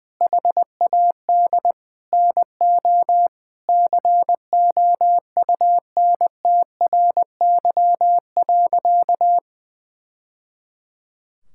Morse code, heard as a keyed tone, about 20 wpm, 700 Hz. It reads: HAD NO COUNTRY.
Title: Sentences from Top 1000 Words - Review Part 5 - 20wpm